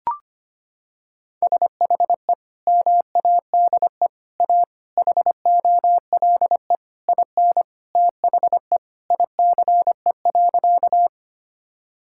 Morse code, heard as a keyed tone, about 25 wpm, 700 Hz. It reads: SHE MADE A HOLE IN THE ICE.
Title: Sentences from Top 1600 Words 25wpm